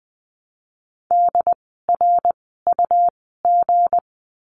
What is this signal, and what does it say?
Morse code, heard as a keyed tone, about 20 wpm, spaced as for 15 wpm, 700 Hz. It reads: DRUG
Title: 160 Lesson 26f - Letter g - Words 20wpm